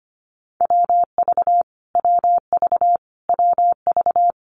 Morse code, heard as a keyed tone, about 25 wpm, 700 Hz. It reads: WV WV WV
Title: US States Repeated 3x 25wpm